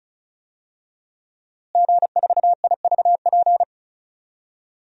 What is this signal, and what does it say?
Morse code, heard as a keyed tone, about 36 wpm, 700 Hz. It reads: G4IVP
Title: DXCC-100 Call Signs - 5 Characters - Repeated 3x - 35wpm